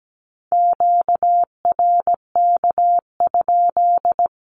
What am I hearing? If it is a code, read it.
Morse code, heard as a keyed tone, about 17 wpm, 700 Hz. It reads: QRK?
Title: Q-Codes 17wpm